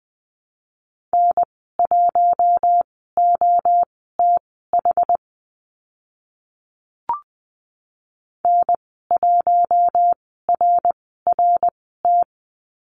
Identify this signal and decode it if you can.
Morse code, heard as a keyed tone, about 20 wpm, spaced as for 14 wpm, 700 Hz. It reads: N1OTH N1RRT
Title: 069 Lesson 12g - Letter d - Call Signs 20wpm